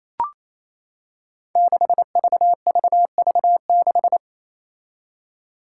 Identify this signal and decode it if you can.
Morse code, heard as a keyed tone, about 28 wpm, 700 Hz. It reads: 6VVV6
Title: Five Challenging Letter-Number Combinations 28wpm